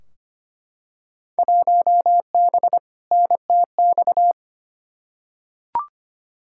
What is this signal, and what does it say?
Morse code, heard as a keyed tone, about 25 wpm, 700 Hz. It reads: 1B NTX
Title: Field Day Exchanges - 25wpm